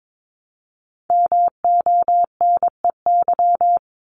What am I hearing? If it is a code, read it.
Morse code, heard as a keyed tone, about 22 wpm, 700 Hz. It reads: MONEY
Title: Multi-letter Phonemes Repeated 3x 22wpm